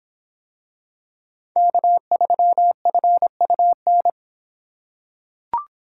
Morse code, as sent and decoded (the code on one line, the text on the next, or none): -.- ...-- ..-. ..- -.
K3FUN